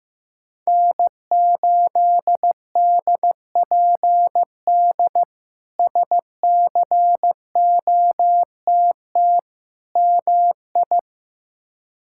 Morse code, heard as a keyed tone, about 15 wpm, 700 Hz. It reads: N8DPD SCOTT MI
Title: K1USN SST QSO Practice - 15wpm